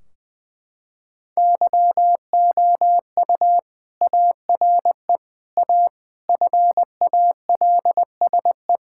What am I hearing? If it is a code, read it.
Morse code, heard as a keyed tone, about 20 wpm, 700 Hz. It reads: YOU ARE A FALSE
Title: Sentences from Top 1600 Words 20wpm